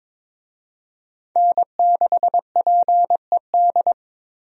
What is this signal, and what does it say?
Morse code, heard as a keyed tone, about 22 wpm, 700 Hz. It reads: N6PED
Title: USA Call Signs - 5 Characters - Repeated 2x - 22wpm